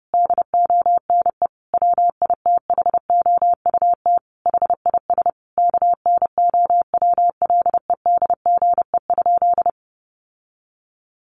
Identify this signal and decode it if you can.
Morse code, heard as a keyed tone, about 30 wpm, 700 Hz. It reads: DONE WITHOUT HIS KNOWLEDGE?
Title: Sentences from Top 600 Words 30wpm